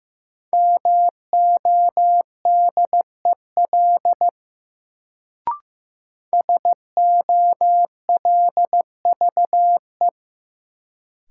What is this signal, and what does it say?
Morse code, heard as a keyed tone, about 15 wpm, 700 Hz. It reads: MODEL SOLVE
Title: Sentences from Top 1000 Words - New Word Drill - 15wpm